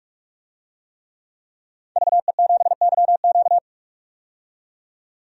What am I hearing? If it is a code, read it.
Morse code, heard as a keyed tone, about 41 wpm, 700 Hz. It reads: VE7YQ
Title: DXCC-100 Call Signs - 5 Characters - 45wpm